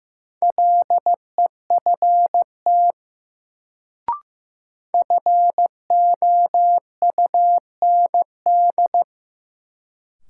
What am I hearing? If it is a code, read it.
Morse code, heard as a keyed tone, about 15 wpm, 700 Hz. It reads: LEFT FOUND
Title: Sentences from Top 300 Words - New Word Drill - 15wpm